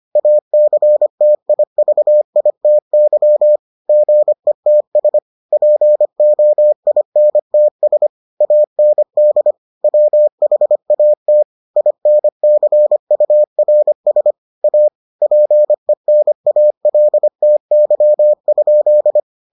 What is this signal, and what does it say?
Morse code, heard as a keyed tone, about 25 wpm, 600 Hz. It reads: ACTIVITY GETS POINTS AND WHAT INCURS A PENALTY?